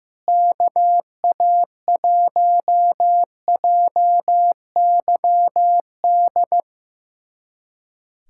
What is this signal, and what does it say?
Morse code, heard as a keyed tone, about 15 wpm, 700 Hz. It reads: KA1JYD
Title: USA Callsign DE Callsign 15wpm